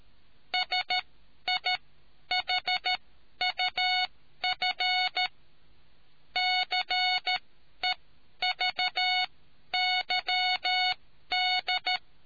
Morse code, heard as a keyed tone, about 13 wpm, spaced as for 10 wpm, 740 Hz. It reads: SIHUF CEVYD